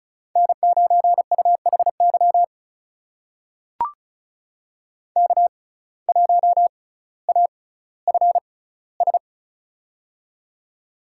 Morse code, heard as a keyed tone, about 36 wpm, spaced as for 18 wpm, 700 Hz. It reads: N9UHY K1AFS
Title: USA Call Signs - 5 Characters - 6x Character Spacing - 35wpm